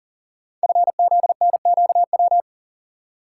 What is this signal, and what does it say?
Morse code, heard as a keyed tone, about 39 wpm, 690 Hz. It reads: FZNQW
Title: Code Groups - 5 Letters - 40wpm